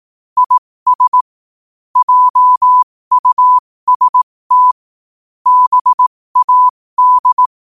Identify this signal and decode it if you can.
Morse code, heard as a keyed tone, about 18 wpm, spaced as for 15 wpm, 1000 Hz. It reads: IS JUST BAD
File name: qotdfiles / 15wpm